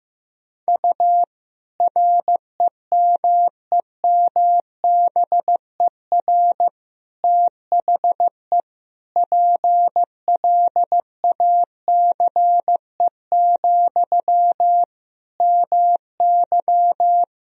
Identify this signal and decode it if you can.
Morse code, heard as a keyed tone, about 15 wpm, 700 Hz. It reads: U REMEMBER THE PLACE, MY
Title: Sentences from Top 400 Words and Abbreviations 15wpm